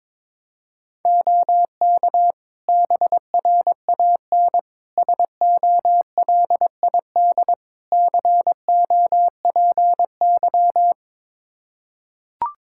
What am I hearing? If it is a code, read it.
Morse code, heard as a keyed tone, about 22 wpm, 700 Hz. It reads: OK BRAN SOLID COPY
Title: Practice QSO - Exchange 3 - More Info - 22wpm